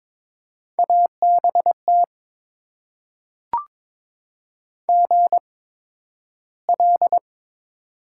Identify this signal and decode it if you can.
Morse code, heard as a keyed tone, about 22 wpm, spaced as for 12 wpm, 700 Hz. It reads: ABT GL
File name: Top 90 QSO Elements - 8x Character Spacing - 22wpm